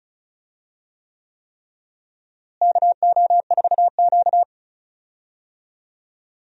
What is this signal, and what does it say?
Morse code, heard as a keyed tone, about 36 wpm, 700 Hz. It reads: KO4Q
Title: USA Call Signs - 4 Characters - Repeated 3x - 35wpm